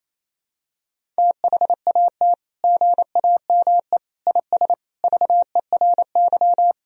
Morse code, heard as a keyed tone, about 28 wpm, 700 Hz. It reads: THAT GAME IS VERY
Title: Sentences from Top 1000 Words - Review Part 3 - 28wpm